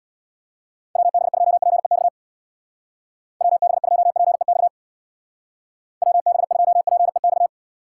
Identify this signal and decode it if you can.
Morse code, heard as a keyed tone, about 61 wpm, 700 Hz. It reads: WB2PEX WB2PEX WB2PEX
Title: Speed Racing - USA 2x3 Call Signs 50wpm